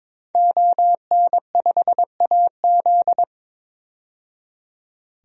Morse code, heard as a keyed tone, about 22 wpm, 700 Hz. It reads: ON5AZ
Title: DXCC-100 Call Signs - 5 Characters - 22wpm